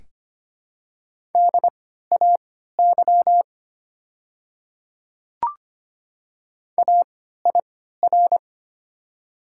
Morse code, heard as a keyed tone, about 25 wpm, spaced as for 14 wpm, 690 Hz. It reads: DAY AIR
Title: 3x Character Spacing - Top 200 Words - 25wpm